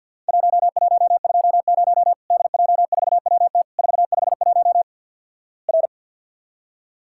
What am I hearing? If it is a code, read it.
Morse code, heard as a keyed tone, about 50 wpm, 700 Hz. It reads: 1120 DJ4WT 451 R
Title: Worked All Europe DX Contest - QTC Exchange - 50wpm